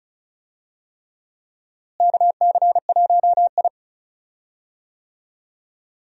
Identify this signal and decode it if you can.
Morse code, heard as a keyed tone, about 36 wpm, 700 Hz. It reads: KC1I
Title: USA Call Signs - 4 Characters - Repeated 3x - 35wpm